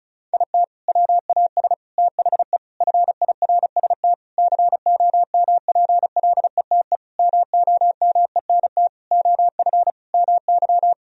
Morse code, heard as a keyed tone, about 36 wpm, 700 Hz. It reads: IT WAS THE FIRST COMPLETE MOMENT OF MY
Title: Sentences from Top 1000 Words - Review Part 2 - 35wpm